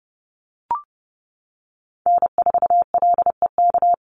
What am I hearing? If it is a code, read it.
Morse code, heard as a keyed tone, about 30 wpm, 700 Hz. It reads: N4LEK
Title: USA Call Signs 30wpm